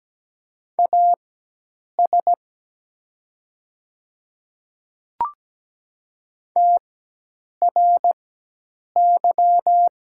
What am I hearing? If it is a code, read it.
Morse code, heard as a keyed tone, about 17 wpm, spaced as for 8 wpm, 700 Hz. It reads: AS TRY
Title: 4x Character Spacing - Top 200 Words - 17wpm